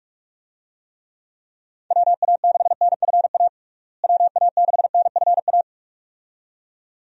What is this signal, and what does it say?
Morse code, heard as a keyed tone, about 43 wpm, 700 Hz. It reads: WA6NFA WA6NFA
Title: USA Call Signs - 6 Characters - Repeated 2x - 45wpm